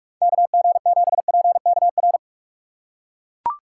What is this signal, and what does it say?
Morse code, heard as a keyed tone, about 41 wpm, 700 Hz. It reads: KG7PKR
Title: USA Call Signs - 3x Character Spacing - 45wpm